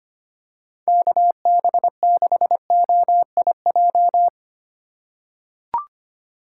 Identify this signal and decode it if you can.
Morse code, heard as a keyed tone, about 25 wpm, 700 Hz. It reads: KB6OIJ
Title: USA Call Signs - 6 Characters - 4x Character Spacing - 25wpm